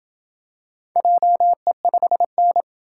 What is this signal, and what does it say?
Morse code, heard as a keyed tone, about 28 wpm, 700 Hz. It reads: JE5N